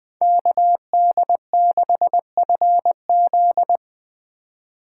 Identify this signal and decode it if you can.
Morse code, heard as a keyed tone, about 20 wpm, 700 Hz. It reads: KD6FZ
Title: Speed Racing - USA 2x2 Call Signs 17wpm